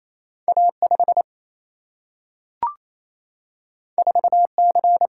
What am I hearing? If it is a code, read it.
Morse code, heard as a keyed tone, about 28 wpm, 700 Hz. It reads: A 5 4 C
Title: Single-Letter Single-Number 28wpm